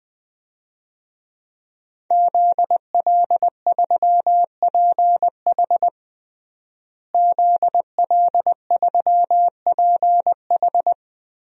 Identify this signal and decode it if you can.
Morse code, heard as a keyed tone, about 20 wpm, 700 Hz. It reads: ZL3PH ZL3PH
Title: DXCC-100 Call Signs - 5 Characters - Repeated 2x - 20wpm